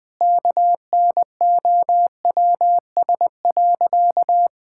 Morse code, heard as a keyed tone, about 20 wpm, 700 Hz. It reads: KNOWS.
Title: Sentences with 5-Word Phrases - Spoken Beginning - 20wpm